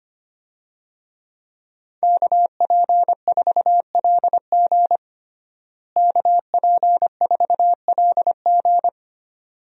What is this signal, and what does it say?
Morse code, heard as a keyed tone, about 25 wpm, 700 Hz. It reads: KP4LG KP4LG
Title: DXCC-100 Call Signs - 5 Characters - Repeated 2x - 25wpm